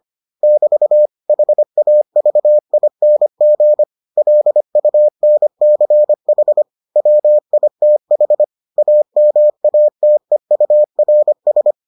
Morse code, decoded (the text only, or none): <BT> HAVING LUNCH WITH AMATEURS